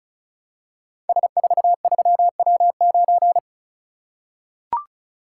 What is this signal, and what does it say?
Morse code, heard as a keyed tone, about 36 wpm, 700 Hz. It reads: S43W9
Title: Code Groups - 5 Letter:Numbers - 35wpm